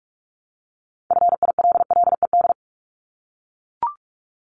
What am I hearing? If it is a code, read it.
Morse code, heard as a keyed tone, about 44 wpm, 700 Hz. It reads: FILLED
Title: ED Suffix - Top 100 Words with Base Words 45wpm